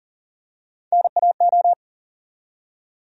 Morse code, heard as a keyed tone, about 40 wpm, 700 Hz. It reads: NAO
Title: USA Call Sign Postfixes - 3 Letters 40wpm